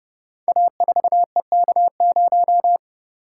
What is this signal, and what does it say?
Morse code, heard as a keyed tone, about 30 wpm, 700 Hz. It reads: A4EK0
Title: Code Groups - 5 Letter:Numbers - 30wpm